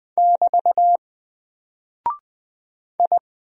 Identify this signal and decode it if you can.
Morse code, heard as a keyed tone, about 20 wpm, 700 Hz. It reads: <BT> I
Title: Single Character (Letter:Number:Punctuation:Prosign) Alt 20wpm